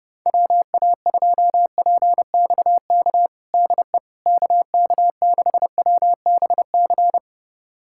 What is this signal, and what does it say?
Morse code, heard as a keyed tone, about 30 wpm, 700 Hz. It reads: WA2PXK DE KK6WBC
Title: USA Callsign DE Callsign - 6 Characters - 30wpm